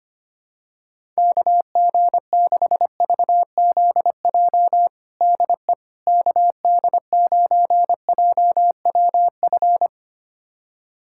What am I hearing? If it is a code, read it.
Morse code, heard as a keyed tone, about 25 wpm, 700 Hz. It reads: KG6VZJ DE KD9JWF
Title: USA Callsign DE Callsign - 6 Characters - 25wpm